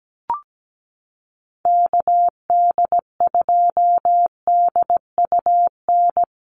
Morse code, heard as a keyed tone, about 17 wpm, 700 Hz. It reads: KD2DUN
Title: USA Call Signs Repeated 2x 17wpm